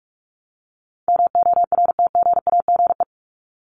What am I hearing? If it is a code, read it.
Morse code, heard as a keyed tone, about 44 wpm, 700 Hz. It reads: MORTGAGE
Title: Top 1751-2000 Words 45wpm